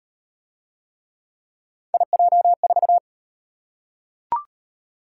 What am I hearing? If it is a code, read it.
Morse code, heard as a keyed tone, about 36 wpm, 700 Hz. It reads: IJ4